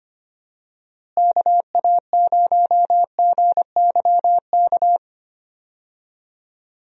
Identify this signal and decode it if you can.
Morse code, heard as a keyed tone, about 25 wpm, 700 Hz. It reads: KA0GYK
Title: USA Call Signs - 6 Characters - Repeated 2x - 25wpm